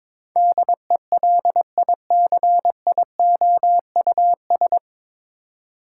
Morse code, heard as a keyed tone, about 21 wpm, 700 Hz. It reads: DELICIOUS
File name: practice / 22wpm / Nine Letter Words - 22wpm